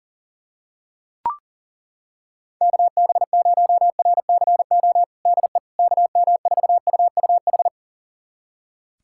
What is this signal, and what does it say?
Morse code, heard as a keyed tone, about 40 wpm, 700 Hz. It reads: KB0RCO DE KM4UUH
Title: USA Callsign DE Callsign - 6 Characters - 40wpm